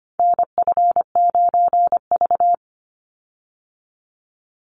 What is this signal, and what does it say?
Morse code, heard as a keyed tone, about 25 wpm, 700 Hz. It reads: NF9V
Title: Call Signs Repeated 2x - 25wpm